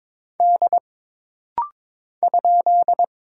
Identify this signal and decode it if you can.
Morse code, heard as a keyed tone, about 22 wpm, 700 Hz. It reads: D ?
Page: This rapid-fire timing is geared to push your ICR (Instant Character Recognition) skills.